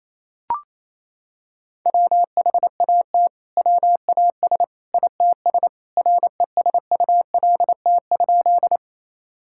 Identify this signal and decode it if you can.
Morse code, heard as a keyed tone, about 28 wpm, 700 Hz. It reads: WHAT WAS ITS RESULT?
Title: Sentences from Top 1000 Words - Review Part 5 - 28wpm